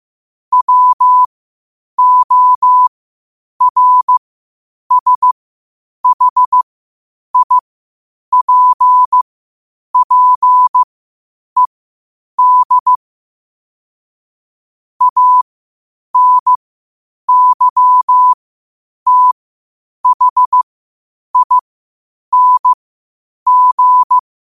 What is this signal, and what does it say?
Morse code, heard as a keyed tone, about 15 wpm, spaced as for 8 wpm, 1000 Hz. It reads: WORSHIPPED ANYTHING
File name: qotdfiles / 8wpm